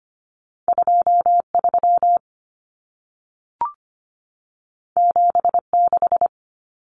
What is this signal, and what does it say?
Morse code, heard as a keyed tone, about 25 wpm, 700 Hz. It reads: 23 76
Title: Numbers - 2-Digit 25wpm